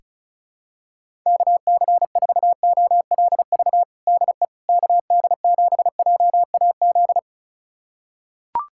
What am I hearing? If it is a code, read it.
Morse code, heard as a keyed tone, about 36 wpm, 700 Hz. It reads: KC4OLV DE KD7JAZ
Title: USA Callsign DE Callsign - 6 Characters - 35wpm